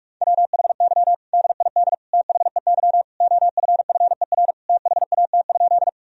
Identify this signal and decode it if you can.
Morse code, heard as a keyed tone, about 46 wpm, 700 Hz. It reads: WHY DID THEY OFFER THAT?
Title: Sentences from Top 1000 Words - Review Part 4 - 45wpm